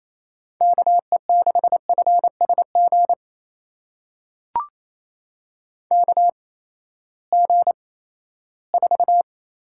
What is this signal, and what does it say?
Morse code, heard as a keyed tone, about 28 wpm, spaced as for 16 wpm, 700 Hz. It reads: KE6FSG KG4
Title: USA Call Signs - 6 Characters - 8x Character Spacing - 28wpm